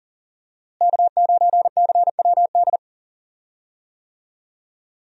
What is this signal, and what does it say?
Morse code, heard as a keyed tone, about 40 wpm, 700 Hz. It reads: K9CWD
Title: USA Call Signs - 5 Characters - Repeated 2x - 40wpm